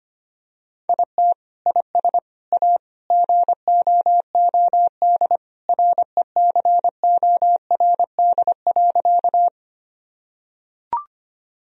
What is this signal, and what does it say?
Morse code, heard as a keyed tone, about 25 wpm, 700 Hz. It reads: IT IS A GOOD RECORD.
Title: Sentences from Top 1000 Words - Review Part 3 - 25wpm